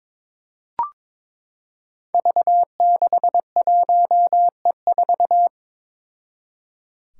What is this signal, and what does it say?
Morse code, heard as a keyed tone, about 22 wpm, 700 Hz. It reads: V61E4
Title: Five Challenging Letter-Number Combinations 22wpm